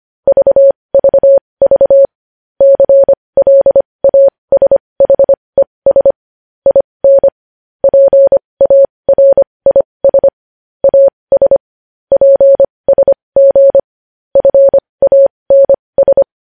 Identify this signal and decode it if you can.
Morse code, heard as a keyed tone, about 25 wpm, spaced as for 20 wpm, 550 Hz. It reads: VVV CLASHES IN PARIS AS PSG FANS
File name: podcast / News headlines-1x-25x20